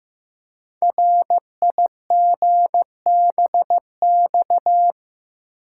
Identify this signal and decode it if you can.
Morse code, heard as a keyed tone, about 15 wpm, 700 Hz. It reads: RIGBX
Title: Code Groups - 5 Letters - 15wpm